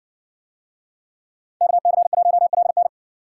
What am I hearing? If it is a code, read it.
Morse code, heard as a keyed tone, about 60 wpm, 700 Hz. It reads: KC1LN